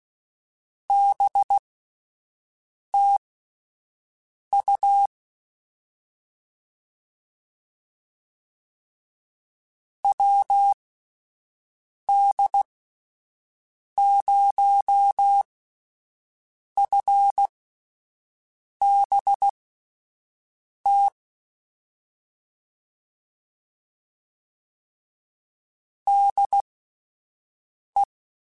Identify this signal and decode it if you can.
Morse code, heard as a keyed tone, about 16 wpm, spaced as for 5 wpm, 800 Hz. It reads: BT U WD0FBT DE
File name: QSOs / QSOs-mp3s / WD0FBT 1@10